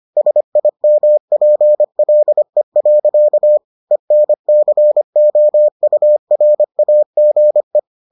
Morse code, heard as a keyed tone, about 25 wpm, 600 Hz. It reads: SIMPLE. ENCOURAGE